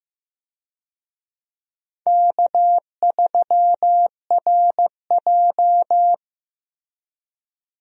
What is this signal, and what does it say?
Morse code, heard as a keyed tone, about 15 wpm, 700 Hz. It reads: K3RJ
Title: USA Call Signs - 4 Characters - Repeated 3x - 15wpm